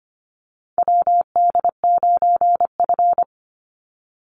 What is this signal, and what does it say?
Morse code, heard as a keyed tone, about 25 wpm, 700 Hz. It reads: WD9F
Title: NA SOTA Chaser Callsigns 25wpm